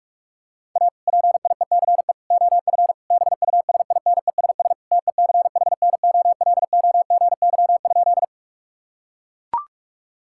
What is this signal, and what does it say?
Morse code, heard as a keyed tone, about 42 wpm, 700 Hz. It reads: A PIECE OF BUSINESS TECHNOLOGY?
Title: Sentences from Top 1000 Words - Review Part 5 - 45wpm